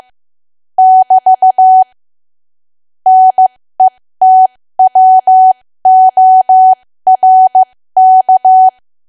Morse code, heard as a keyed tone, about 15 wpm, spaced as for 13 wpm, 750 Hz. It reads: <BT> NETWORK
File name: cwa / W1AW-Level2 / 13WPM